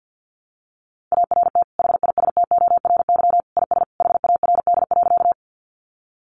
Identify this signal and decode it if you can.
Morse code, heard as a keyed tone, about 50 wpm, 700 Hz. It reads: ART HISTORY IS HARD.
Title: Sentences from Top 600 Words 50wpm